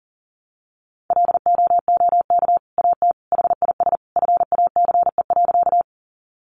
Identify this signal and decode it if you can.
Morse code, heard as a keyed tone, about 40 wpm, 700 Hz. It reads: LOOK AT HIS FACE.
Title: Sentences from Top 300 Words 40wpm